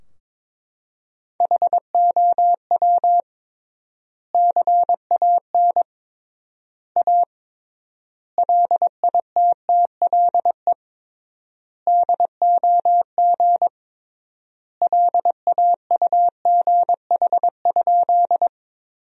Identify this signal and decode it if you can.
Morse code, heard as a keyed tone, about 21 wpm, 700 Hz. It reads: HOW CAN A LITTLE DOG LAUGH?
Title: 3x Word-Spacing - Sentences from Top 500 Words - 22wpm